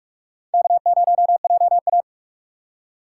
Morse code, heard as a keyed tone, about 42 wpm, 700 Hz. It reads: K0JA
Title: USA Call Signs - 2x Character Spacing - 45wpm